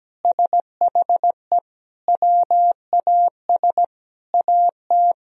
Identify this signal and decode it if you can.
Morse code, heard as a keyed tone, about 17 wpm, 700 Hz. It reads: SHE WAS AT
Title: Sentences from Top 1000 Words - Review Part 3 - 17wpm